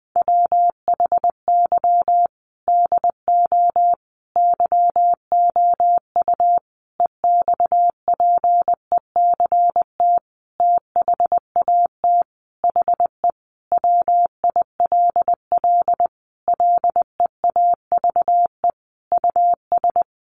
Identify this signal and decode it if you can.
Morse code, heard as a keyed tone, about 20 wpm, 700 Hz. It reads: WHY DO YOU EXPECT THAT HE WILL LEAVE US
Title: Sentences from Top 1100 Words 20wpm